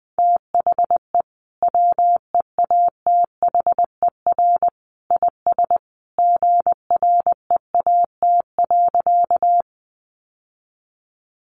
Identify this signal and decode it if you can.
Morse code, heard as a keyed tone, about 20 wpm, 700 Hz. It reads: THE WEATHER IS GREAT.
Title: Sentences from Top 800 Words 20wpm